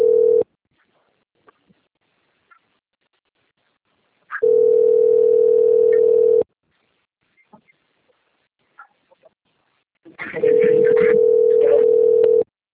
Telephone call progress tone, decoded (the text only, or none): ringback tone